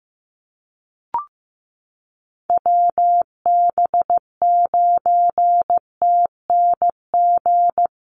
Morse code, heard as a keyed tone, about 15 wpm, 700 Hz. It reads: WB9TNG
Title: USA Call Signs Repeated 3x 15wpm